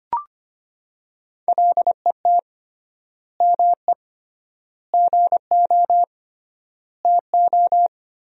Morse code, heard as a keyed tone, about 25 wpm, 700 Hz. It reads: LET ME GO TO
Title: 3x Word-Spacing - Sentences from Top 300 Words - 25wpm